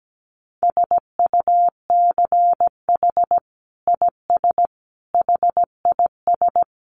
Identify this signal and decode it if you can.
Morse code, heard as a keyed tone, about 17 wpm, 700 Hz. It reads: SUCH IS HIS
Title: Sentences from Top 900 Words 17wpm